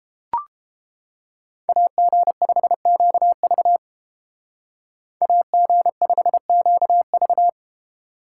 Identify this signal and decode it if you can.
Morse code, heard as a keyed tone, about 29 wpm, 700 Hz. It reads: AG5QV AG5QV